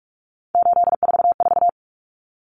Morse code, heard as a keyed tone, about 44 wpm, 700 Hz. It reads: 844
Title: Numbers - 3-Digit 45wpm